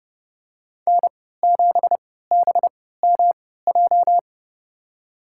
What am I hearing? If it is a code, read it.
Morse code, heard as a keyed tone, about 30 wpm, spaced as for 17 wpm, 700 Hz. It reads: N7BMJ
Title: USA Call Signs - 5 Characters - 3x Character Spacing - 30wpm